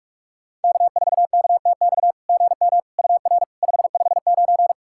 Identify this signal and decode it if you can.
Morse code, heard as a keyed tone, about 46 wpm, 690 Hz. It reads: K4KTX GM UR 559
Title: SOTA QSO Practice 45wpm